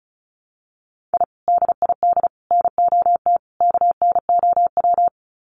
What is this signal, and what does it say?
Morse code, heard as a keyed tone, about 34 wpm, 700 Hz. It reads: I DID NOT KNOW
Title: Sets of 4 Words 35wpm